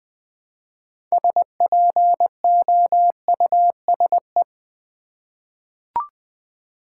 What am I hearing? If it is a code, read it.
Morse code, heard as a keyed tone, about 20 wpm, 700 Hz. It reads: SPOUSE
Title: 6 Letter Words - First 3 Letters Given - 20wpm